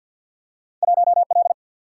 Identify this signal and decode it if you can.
Morse code, heard as a keyed tone, about 53 wpm, 700 Hz. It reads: 1L
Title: Single-Letter Single-Number 50wpm